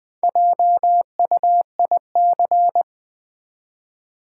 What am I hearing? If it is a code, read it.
Morse code, heard as a keyed tone, about 21 wpm, 700 Hz. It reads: JUIC
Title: Build A Word - 1 Letter At A Time - 3 to 5-Letter Words - 20wpm